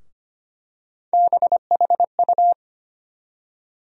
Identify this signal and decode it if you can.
Morse code, heard as a keyed tone, about 25 wpm, 700 Hz. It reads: BHU
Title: USA Call Sign Postfixes - 3 Letters 25wpm